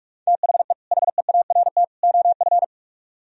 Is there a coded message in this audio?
Yes: Morse code from 0.3 to 2.6 s, heard as a keyed tone, about 44 wpm, 700 Hz. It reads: THE HEART OF